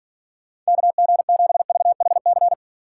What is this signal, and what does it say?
Morse code, heard as a keyed tone, about 52 wpm, 700 Hz. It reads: KG7VHC